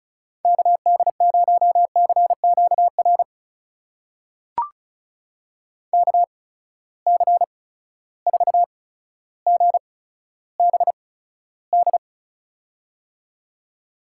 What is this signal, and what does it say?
Morse code, heard as a keyed tone, about 36 wpm, spaced as for 15 wpm, 700 Hz. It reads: KD0CQR KC4GBD
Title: USA Call Signs - 6 Characters - 8x Character Spacing - 35wpm